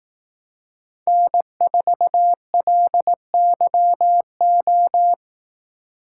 Morse code, heard as a keyed tone, about 17 wpm, 700 Hz. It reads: N4LYO
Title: Speed Racing - USA 1x3 Call Signs 15wpm